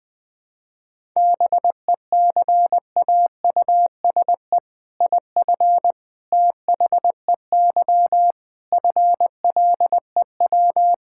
Morse code, heard as a keyed tone, about 20 wpm, 700 Hz. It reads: BECAUSE IF THEY FLEW
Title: All-Age Appropriate Jokes - 20wpm